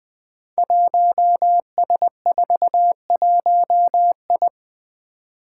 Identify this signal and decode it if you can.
Morse code, heard as a keyed tone, about 20 wpm, 700 Hz. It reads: 1S41I
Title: Five Challenging Letter-Number Combinations 20wpm